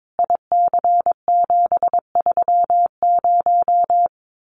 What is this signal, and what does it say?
Morse code, heard as a keyed tone, about 22 wpm, 700 Hz. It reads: IC730
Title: Common QSO Phrases 22wpm